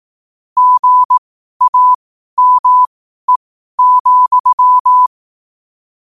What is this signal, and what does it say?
Morse code, heard as a keyed tone, about 18 wpm, spaced as for 12 wpm, 1000 Hz. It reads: GAME,